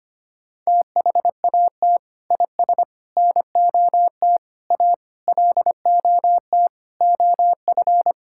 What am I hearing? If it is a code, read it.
Morse code, heard as a keyed tone, about 25 wpm, 700 Hz. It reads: THAT IS NOT A LOT OF
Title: Sentences from Top 1000 Words - Review Part 5 - 25wpm